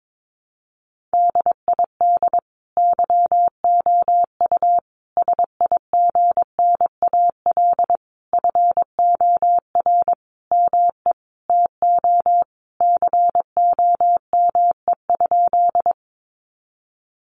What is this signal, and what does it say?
Morse code, heard as a keyed tone, about 22 wpm, 700 Hz. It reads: DID YOU SIGNAL FOR ME TO COME?
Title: Sentences from Top 1400 Words 22wpm